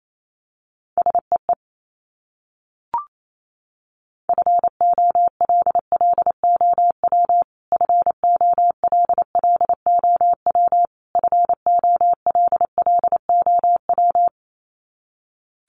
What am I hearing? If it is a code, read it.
Morse code, heard as a keyed tone, about 28 wpm, 700 Hz. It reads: SEE FOLLOW FOLLOW FOLLOW
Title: Top 500 Words Repeated 28wpm